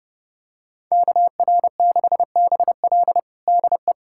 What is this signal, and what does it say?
Morse code, heard as a keyed tone, about 32 wpm, 700 Hz. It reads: KR6BL DE
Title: USA Callsign DE Callsign - 5 Characters - 30wpm